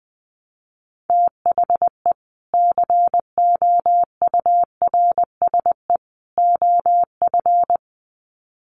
Morse code, heard as a keyed tone, about 19 wpm, 700 Hz. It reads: THE COURSE OF
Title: Sets of 3 Words 20wpm